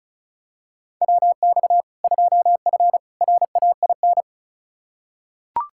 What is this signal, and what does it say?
Morse code, heard as a keyed tone, about 34 wpm, 700 Hz. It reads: WX 2F RAIN
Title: K1USN SST QSO Practice 35wpm